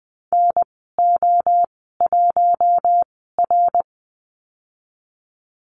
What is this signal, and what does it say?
Morse code, heard as a keyed tone, about 20 wpm, spaced as for 14 wpm, 700 Hz. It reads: NO1R
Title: 062 Lesson 11g - Letter h - Call Signs 20wpm